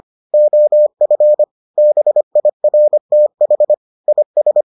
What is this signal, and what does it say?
Morse code, heard as a keyed tone, about 25 wpm, 600 Hz. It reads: OF BIRTH IS